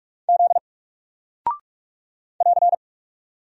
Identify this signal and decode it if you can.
Morse code, heard as a keyed tone, about 46 wpm, 700 Hz. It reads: Z <AR>
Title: Single Character (Letter:Number:Punctuation:Prosign) 45wpm